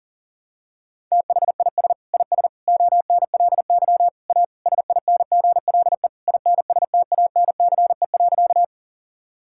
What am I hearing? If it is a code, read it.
Morse code, heard as a keyed tone, about 40 wpm, 700 Hz. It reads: THIS IS ONLY A SINGLE INSTANCE.
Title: Sentences from Top 1000 Words - Review Part 5 - 40wpm